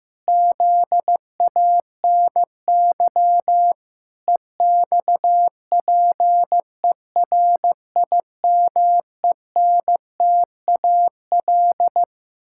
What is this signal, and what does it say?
Morse code, heard as a keyed tone, about 15 wpm, 700 Hz. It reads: ZANY EXPERIMENTAL
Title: Pangram Challenge v2 - 15wpm